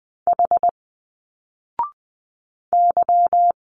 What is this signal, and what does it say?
Morse code, heard as a keyed tone, about 20 wpm, 700 Hz. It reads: H Y
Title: Single Letters 20wpm